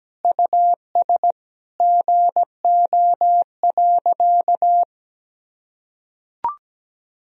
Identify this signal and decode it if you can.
Morse code, heard as a keyed tone, about 17 wpm, 700 Hz. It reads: US GO.
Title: Sentences from Top 400 Words and Abbreviations 17wpm